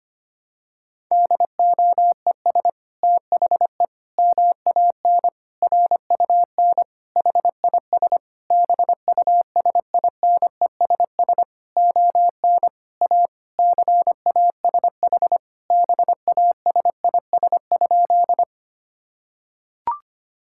Morse code, heard as a keyed tone, about 25 wpm, 700 Hz. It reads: DOES THE MAN RUN HIS BUSINESS ON A CASH BASIS?
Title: Sentences from Top 1000 Words - Review Part 5 - 25wpm